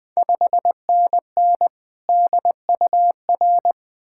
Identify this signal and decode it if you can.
Morse code, heard as a keyed tone, about 20 wpm, 700 Hz. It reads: 5NN DUR
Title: Ontario QSO Party Exchange 20wpm